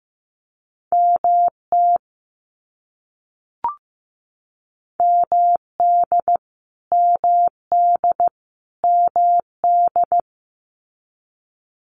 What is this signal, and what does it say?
Morse code, heard as a keyed tone, about 15 wpm, 700 Hz. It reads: MT MD MD MD
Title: US States Repeated 3x 15wpm